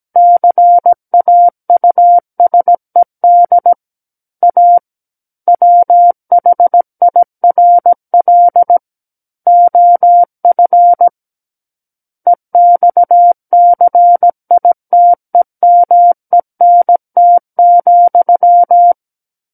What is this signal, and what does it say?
Morse code, heard as a keyed tone, about 17 wpm, 700 Hz. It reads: CAUSED A WHIRL OF EXCITEMENT,
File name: Book One / 08-Chapter 8 - 17 WPM